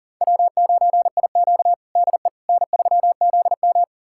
Morse code, heard as a keyed tone, about 40 wpm, 700 Hz. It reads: W9IQ DE N3ZM
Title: USA Callsign DE Callsign - 4 Characters - 40wpm